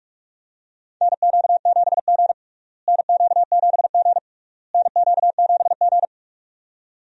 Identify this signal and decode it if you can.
Morse code, heard as a keyed tone, about 42 wpm, 700 Hz. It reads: NQ7G NQ7G NQ7G
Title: USA Call Signs - 4 Characters - Repeated 3x - 45wpm